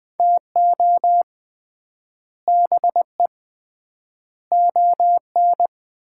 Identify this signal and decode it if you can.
Morse code, heard as a keyed tone, about 20 wpm, 700 Hz. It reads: TO BE ON
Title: 3x Word-Spacing - Sets of 4 Words - 20wpm